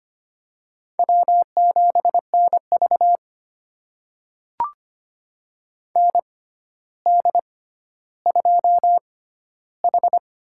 Morse code, heard as a keyed tone, about 25 wpm, spaced as for 13 wpm, 700 Hz. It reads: W7NV ND2H
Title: USA Call Signs - 4 Characters - 6x Character Spacing - 25wpm